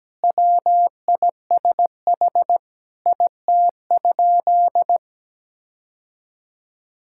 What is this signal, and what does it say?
Morse code, heard as a keyed tone, about 17 wpm, 700 Hz. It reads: WISH IT?
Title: Sentences from Top 1600 Words 17wpm